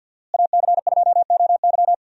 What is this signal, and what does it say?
Morse code, heard as a keyed tone, about 52 wpm, 700 Hz. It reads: AC2OY
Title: USA Call Signs - 5 Characters - Repeated 2x - 50wpm